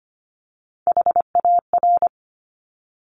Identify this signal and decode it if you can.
Morse code, heard as a keyed tone, about 25 wpm, 700 Hz. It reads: HAR
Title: Top Three Letter Combinations in Words 25wpm